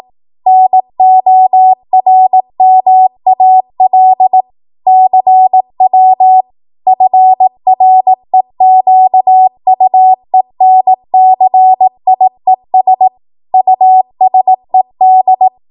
Morse code, heard as a keyed tone, about 18 wpm, 750 Hz. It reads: NORMAL CW FREQUENCIES USED
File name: Archive / 18 WPM / 18WPM